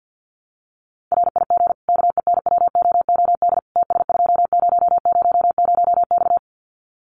Morse code, heard as a keyed tone, about 50 wpm, 700 Hz. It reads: RIG KENWOOD TS2000X
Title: Common QSO Phrases 50wpm